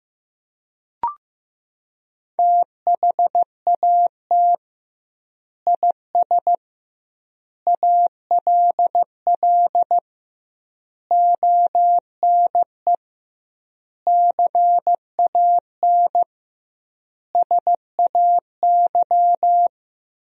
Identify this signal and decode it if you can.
Morse code, heard as a keyed tone, about 15 wpm, 700 Hz. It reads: THAT IS ALL ONE CAN SAY
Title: 2x Word-Spacing - Sentences from Top 200 Words - 15wpm